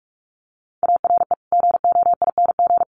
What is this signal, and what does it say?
Morse code, heard as a keyed tone, about 44 wpm, 700 Hz. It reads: ARE GOING
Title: Sets of 3 Words 45wpm